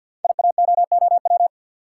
Morse code, heard as a keyed tone, about 46 wpm, 700 Hz. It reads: IAOOW
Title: Code Groups - 5 Letters - 50wpm